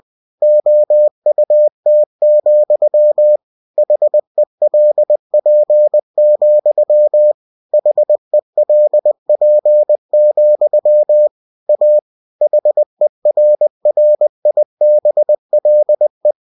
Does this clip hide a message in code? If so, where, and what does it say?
Yes: Morse code from 0.4 to 16.3 s, heard as a keyed tone, about 20 wpm, 600 Hz. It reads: OUT, HELP, HELP, A HERRIBLE